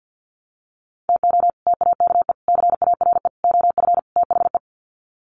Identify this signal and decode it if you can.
Morse code, heard as a keyed tone, about 49 wpm, 700 Hz. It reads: TO TAKE CARE OF THE
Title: Sets of 5 Words Encore 50wpm